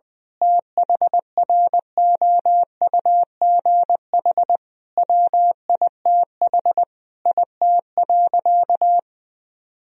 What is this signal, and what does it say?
Morse code, heard as a keyed tone, about 20 wpm, 700 Hz. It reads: THROUGH WITH IT.
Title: Sentences from Top 1000 Words - Review Part 3 - 20wpm